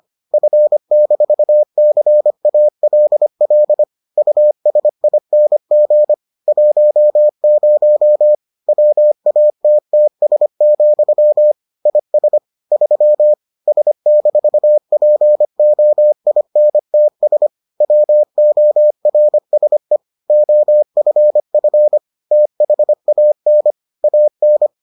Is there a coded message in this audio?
Yes: Morse code from 0.3 to 24.7 s, heard as a keyed tone, about 25 wpm, 600 Hz. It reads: F-CALL USING 10 WATTS, IS 3 S-POINTS WORSE OFF THAN AN